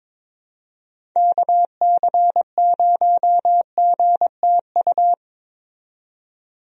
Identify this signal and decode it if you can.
Morse code, heard as a keyed tone, about 22 wpm, 700 Hz. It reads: KC0GTU
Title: USA Call Signs - 6 Characters - 22wpm